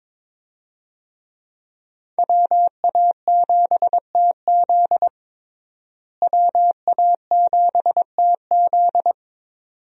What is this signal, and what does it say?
Morse code, heard as a keyed tone, about 22 wpm, 700 Hz. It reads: WA7TZ WA7TZ
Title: USA Call Signs - 5 Characters - Repeated 2x - 22wpm